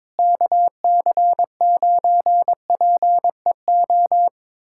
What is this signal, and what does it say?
Morse code, heard as a keyed tone, about 22 wpm, 700 Hz. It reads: KC9PEO
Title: USA Callsign DE Callsign 22wpm